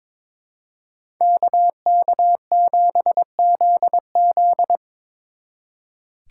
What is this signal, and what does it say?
Morse code, heard as a keyed tone, about 26 wpm, 700 Hz. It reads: KK7ZZ
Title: Speed Racing - USA 2x2 Call Signs 22wpm